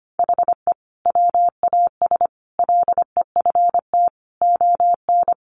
Morse code, heard as a keyed tone, about 25 wpm, 700 Hz. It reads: HE WAS LEFT ON
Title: Sentences from Top 700 Words 25wpm